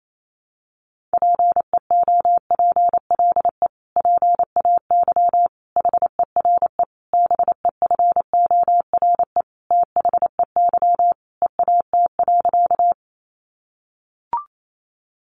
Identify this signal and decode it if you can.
Morse code, heard as a keyed tone, about 28 wpm, 700 Hz. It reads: PEOPLE PAY HERE BEFORE THEY EAT.
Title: Sentences from Top 700 Words 28wpm